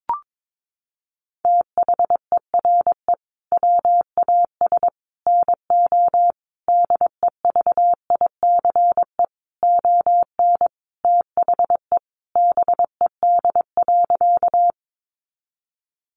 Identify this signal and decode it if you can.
Morse code, heard as a keyed tone, about 22 wpm, 700 Hz. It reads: THERE WAS NO DEVICE ON THE BED.
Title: Sentences from Top 1500 Words 22wpm